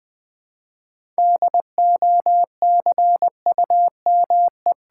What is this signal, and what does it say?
Morse code, heard as a keyed tone, about 21 wpm, 700 Hz. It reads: DOCUME
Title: Build A Word - 2 Letters At A Time - 8-Letter Words - 20wpm